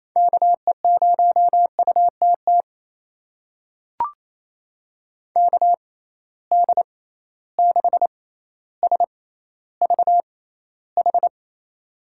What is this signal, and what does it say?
Morse code, heard as a keyed tone, about 28 wpm, spaced as for 14 wpm, 700 Hz. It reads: KE0UTT KD6SVH
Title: USA Call Signs - 6 Characters - 6x Character Spacing - 28wpm